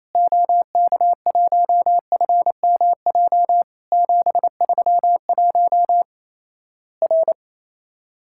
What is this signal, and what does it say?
Morse code, heard as a keyed tone, about 27 wpm, 700 Hz. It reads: OK1FMJ 731 R
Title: Worked All Europe DX Contest - QTC Exchange - 28wpm